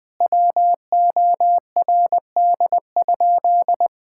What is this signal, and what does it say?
Morse code, heard as a keyed tone, about 21 wpm, 700 Hz. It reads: WORD?
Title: Sentences from Top 1000 Words - Review Part 6 - 20wpm